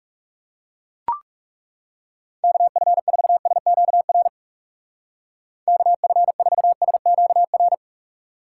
Morse code, heard as a keyed tone, about 37 wpm, 700 Hz. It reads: KF4SQR KF4SQR
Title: Speed Racing - USA 2x3 Call Signs 30wpm